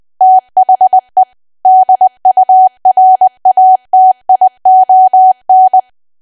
Morse code, heard as a keyed tone, about 20 wpm, 750 Hz. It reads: THE DURATION
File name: cwa / W1AW-Level3 / 20WPM